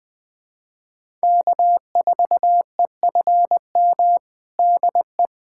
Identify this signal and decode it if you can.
Morse code, heard as a keyed tone, about 20 wpm, 700 Hz. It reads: K4EFM DE
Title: USA Callsign DE Callsign - 5 Characters - 20wpm